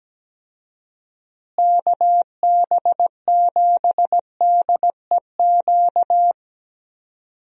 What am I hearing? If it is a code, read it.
Morse code, heard as a keyed tone, about 17 wpm, 700 Hz. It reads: KB7DEQ
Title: USA Call Signs - 6 Characters - Repeated 2x - 17wpm